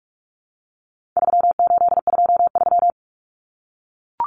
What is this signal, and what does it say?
Morse code, heard as a keyed tone, about 44 wpm, 700 Hz. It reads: 3823
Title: Common Cut Numbers - 4-Digit 45wpm